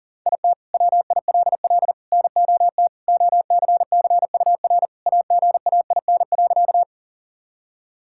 Morse code, heard as a keyed tone, about 40 wpm, 700 Hz. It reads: IT WILL NOT OCCUR AGAIN.
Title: Sentences from Top 1000 Words - Review Part 6 - 40wpm